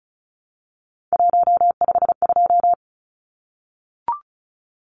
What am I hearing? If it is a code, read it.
Morse code, heard as a keyed tone, about 34 wpm, 700 Hz. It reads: 152
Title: Numbers - 3-Digit 35wpm